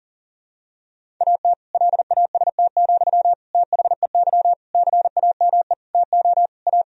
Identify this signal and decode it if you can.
Morse code, heard as a keyed tone, about 37 wpm, 700 Hz. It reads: AT LAST, THEY CAME TO A
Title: Sentences from Top 1000 Words - Review Part 1 - 40wpm